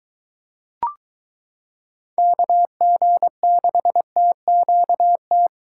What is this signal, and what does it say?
Morse code, heard as a keyed tone, about 23 wpm, 700 Hz. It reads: KG6TQT